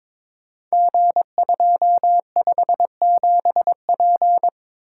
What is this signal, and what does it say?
Morse code, heard as a keyed tone, about 22 wpm, 700 Hz. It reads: G257P
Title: Code Groups - 5 Letter:Numbers - 22wpm